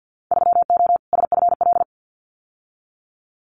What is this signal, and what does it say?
Morse code, heard as a keyed tone, about 48 wpm, 700 Hz. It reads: 3O SFL
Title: Winter Field Day Exchanges 50wpm